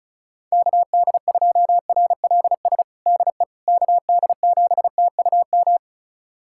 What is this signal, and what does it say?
Morse code, heard as a keyed tone, about 34 wpm, 690 Hz. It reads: KD2RLS DE KD7TUM
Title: USA Callsign DE Callsign - 6 Characters - 35wpm